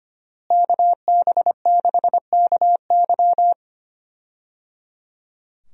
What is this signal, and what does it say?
Morse code, heard as a keyed tone, about 25 wpm, 700 Hz. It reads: KB6KY
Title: USA Call Signs - 5 Characters - Repeated 3x - 25wpm